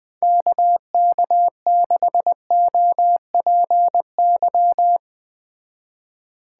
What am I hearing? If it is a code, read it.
Morse code, heard as a keyed tone, about 21 wpm, 700 Hz. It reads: KK6OPY